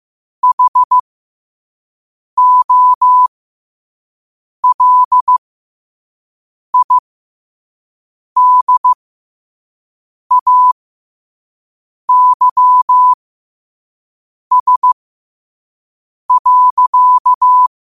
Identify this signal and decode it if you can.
Morse code, heard as a keyed tone, about 15 wpm, spaced as for 5 wpm, 1000 Hz. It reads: HOLIDAYS.
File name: qotdfiles / 5wpm